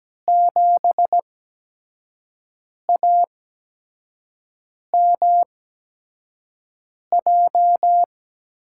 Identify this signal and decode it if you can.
Morse code, heard as a keyed tone, about 17 wpm, spaced as for 5 wpm, 700 Hz. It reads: 7AMJ
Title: USA Call Signs - 6 Characters - 8x Character Spacing - 17wpm